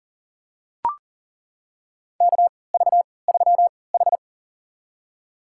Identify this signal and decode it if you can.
Morse code, heard as a keyed tone, about 40 wpm, spaced as for 23 wpm, 700 Hz. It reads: KV3H
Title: USA Call Signs - 3x Character Spacing - 40wpm